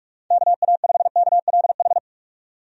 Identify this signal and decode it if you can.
Morse code, heard as a keyed tone, about 46 wpm, 700 Hz. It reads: KA5KLH
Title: USA Call Signs - 6 Characters - 45wpm